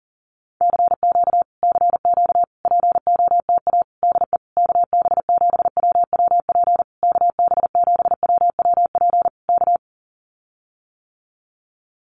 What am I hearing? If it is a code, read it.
Morse code, heard as a keyed tone, about 40 wpm, 700 Hz. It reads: CQ CQ POTA DE KB7WWP KB7WWP K
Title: Common QSO Phrases Encore 40wpm